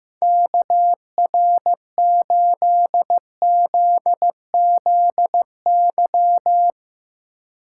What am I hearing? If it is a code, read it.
Morse code, heard as a keyed tone, about 16 wpm, 700 Hz. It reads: KR8ZZY
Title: USA Call Signs - 6 Characters - 4x Character Spacing - 15wpm